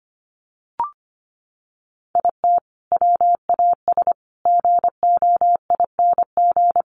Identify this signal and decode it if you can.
Morse code, heard as a keyed tone, about 25 wpm, 700 Hz. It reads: IT WAS GOING
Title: Sets of 5 Words 25wpm